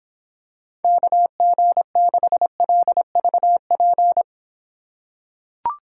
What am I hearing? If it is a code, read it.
Morse code, heard as a keyed tone, about 24 wpm, 700 Hz. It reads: KG6LVP